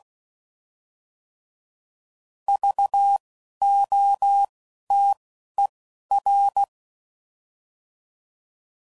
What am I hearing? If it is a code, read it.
Morse code, heard as a keyed tone, about 16 wpm, spaced as for 11 wpm, 800 Hz. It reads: VOTER
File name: Prefixes / By First Letter / Vs